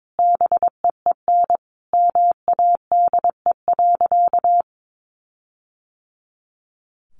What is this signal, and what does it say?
Morse code, heard as a keyed tone, about 22 wpm, 700 Hz. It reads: BEEN MADE.
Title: Sentences from Top 1400 Words 22wpm